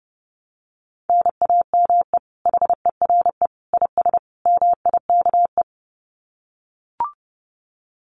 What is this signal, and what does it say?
Morse code, heard as a keyed tone, about 30 wpm, 700 Hz. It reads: NAME HERE IS MIKE
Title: Common QSO Phrases 30wpm